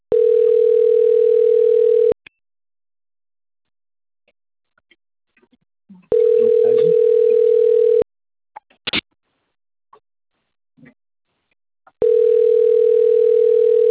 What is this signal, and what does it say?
Ringback tone (North America), 440 + 480 Hz.
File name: out- - - - - .65345